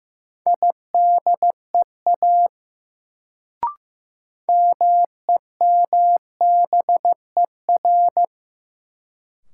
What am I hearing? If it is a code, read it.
Morse code, heard as a keyed tone, about 16 wpm, 700 Hz. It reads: IDEA MEMBER
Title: Sets of 3 Words - New Word Drill - 15wpm